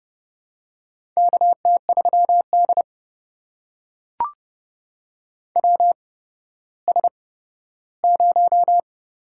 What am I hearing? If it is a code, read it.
Morse code, heard as a keyed tone, about 31 wpm, spaced as for 15 wpm, 700 Hz. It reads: KT3D WS0
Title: USA Call Signs - 4 Characters - 8x Character Spacing - 30wpm